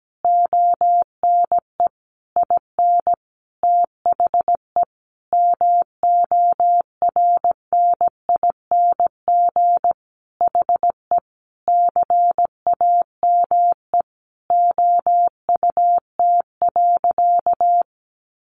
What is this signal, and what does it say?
Morse code, heard as a keyed tone, about 17 wpm, 700 Hz. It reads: ONE IN THE MORNING HE CAME OUT.
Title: Sentences from Top 1300 Words 17wpm